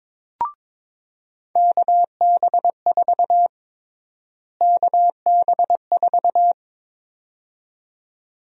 Morse code, heard as a keyed tone, about 22 wpm, 700 Hz. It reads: KB4 KB4
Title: USA Call Sign Prefixes - Repeated 2x - 22wpm